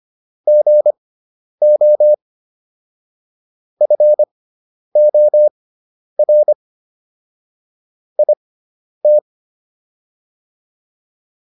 Morse code, heard as a keyed tone, about 25 wpm, spaced as for 10 wpm, 600 Hz. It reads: GO FOR IT